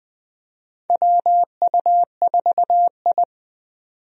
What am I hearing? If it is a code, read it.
Morse code, heard as a keyed tone, about 20 wpm, 700 Hz. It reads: WU4I
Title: USA Call Signs - 4 Characters - 20wpm